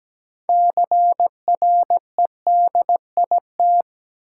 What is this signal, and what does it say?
Morse code, heard as a keyed tone, about 17 wpm, 700 Hz. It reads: CREDIT
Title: Sentences from Top 1000 Words - New Word Drill - 17wpm